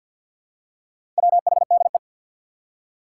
Morse code, heard as a keyed tone, about 53 wpm, 700 Hz. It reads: WHDE
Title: Four Challenging Letter-Number Combinations 50wpm